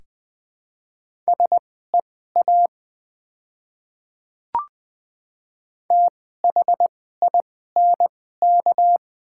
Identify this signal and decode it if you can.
Morse code, heard as a keyed tone, about 20 wpm, spaced as for 15 wpm, 700 Hz. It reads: SEA THINK
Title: 2x Character Spacing - Top 500 Words - 20wpm